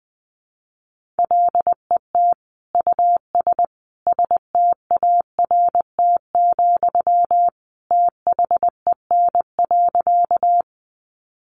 Morse code, heard as a keyed tone, about 20 wpm, 700 Hz. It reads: LET US START, THEN.
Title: Sentences from Top 300 Words 20wpm